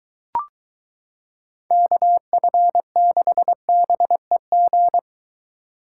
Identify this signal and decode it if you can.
Morse code, heard as a keyed tone, about 22 wpm, 700 Hz. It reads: KF6BEG